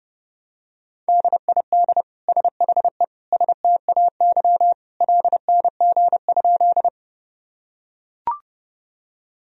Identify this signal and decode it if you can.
Morse code, heard as a keyed tone, about 30 wpm, 700 Hz. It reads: DID SHE STAY LNG?
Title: Sentences from Top 500 Words and Abbreviations 30wpm